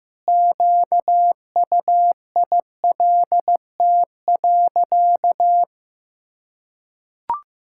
Morse code, heard as a keyed tone, about 15 wpm, 700 Hz. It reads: QUILT.
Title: Pangram Challenge v2 - 15wpm